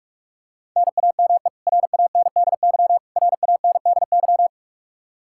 Morse code, heard as a keyed tone, about 46 wpm, 700 Hz. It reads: NAME RANDY RANDY
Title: Practice QSO - Exchange 1 - Beginning - 45wpm